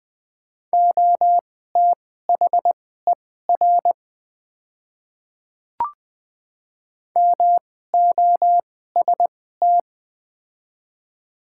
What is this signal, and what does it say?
Morse code, heard as a keyed tone, about 20 wpm, spaced as for 14 wpm, 700 Hz. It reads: OTHER MOST
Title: 2x Character Spacing - Top 200 Words - 20wpm